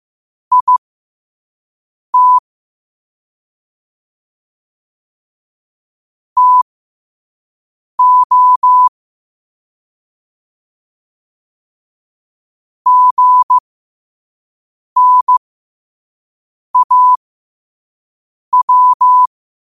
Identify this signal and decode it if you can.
Morse code, heard as a keyed tone, about 15 wpm, spaced as for 5 wpm, 1000 Hz. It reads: IT TO GNAW
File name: qotdfiles / 5wpm